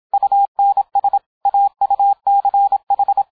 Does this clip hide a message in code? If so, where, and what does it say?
Yes: Morse code from 0.1 to 3.2 s, heard as a keyed tone, about 27 wpm, 800 Hz. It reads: UNS AUCH